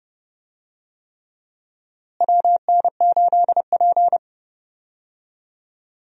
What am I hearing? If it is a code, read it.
Morse code, heard as a keyed tone, about 30 wpm, 700 Hz. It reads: WN8P
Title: USA Call Signs - 4 Characters - Repeated 3x - 30wpm